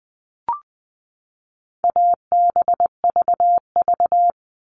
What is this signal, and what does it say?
Morse code, heard as a keyed tone, about 20 wpm, 700 Hz. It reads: ABVV
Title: All Cut Numbers - 4-Digit 20wpm